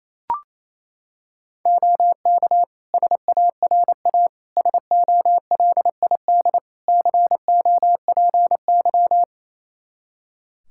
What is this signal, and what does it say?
Morse code, heard as a keyed tone, about 28 wpm, 700 Hz. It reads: OK SARA SOLID COPY
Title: Practice QSO - Exchange 3 - More Info - 28wpm